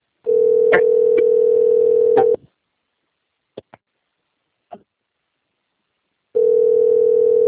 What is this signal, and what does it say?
Ringback tone (North America), 440 + 480 Hz.
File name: rg-1600-unknown- - - .28579